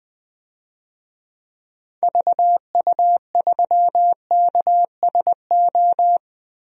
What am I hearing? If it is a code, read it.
Morse code, heard as a keyed tone, about 20 wpm, 700 Hz. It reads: VU3KSO
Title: DXCC-100 Call Signs - 6 Characters - Repeated 2x - 20wpm